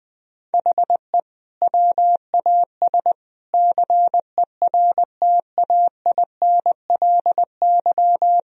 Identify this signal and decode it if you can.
Morse code, heard as a keyed tone, about 20 wpm, 700 Hz. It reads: HE WAS CERTAINLY
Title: Sentences from Top 1000 Words - Review Part 1 - 20wpm